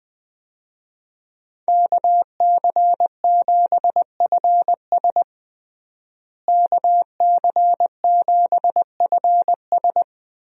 Morse code, heard as a keyed tone, about 21 wpm, 700 Hz. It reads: KC7FS KC7FS
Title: USA Call Signs - 5 Characters - Repeated 3x - 20wpm